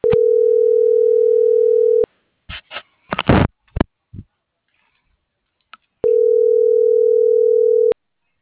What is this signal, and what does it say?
Ringback tone (North America), 440 + 480 Hz.